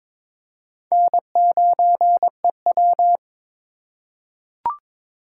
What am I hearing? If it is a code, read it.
Morse code, heard as a keyed tone, about 22 wpm, 700 Hz. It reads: N9EW
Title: USA Call Signs - 4 Characters - 2x Character Spacing - 22wpm